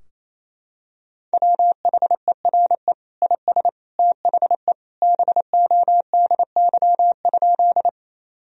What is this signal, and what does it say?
Morse code, heard as a keyed tone, about 28 wpm, 700 Hz. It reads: WHERE IS THE BODY?
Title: Sentences from Top 1000 Words - Review Part 2 - 28wpm